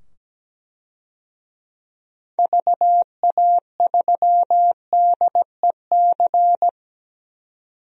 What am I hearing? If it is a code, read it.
Morse code, heard as a keyed tone, about 17 wpm, 700 Hz. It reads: VA3DEC
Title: DXCC-100 Call Signs - 6 Characters - 17wpm